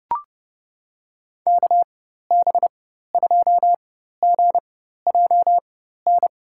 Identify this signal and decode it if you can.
Morse code, heard as a keyed tone, about 32 wpm, spaced as for 14 wpm, 700 Hz. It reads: KB2GJN
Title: USA Call Signs - 6 Characters - 4x Character Spacing - 30wpm